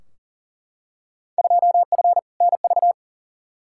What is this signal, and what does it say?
Morse code, heard as a keyed tone, about 40 wpm, 700 Hz. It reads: 2F NV
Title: Field Day Exchanges - 40wpm